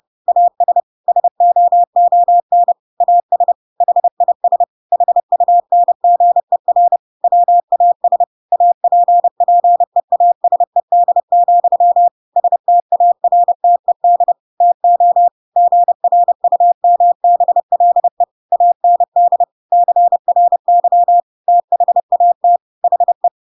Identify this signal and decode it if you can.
Morse code, heard as a keyed tone, about 30 wpm, 700 Hz. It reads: AS SOON AS HIS HUNGER WAS APPEASED, STARTED TO GRUMBLE AND CRY THAT HE
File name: Pinocchio / Pinocchio ch8 30 30 ik1hge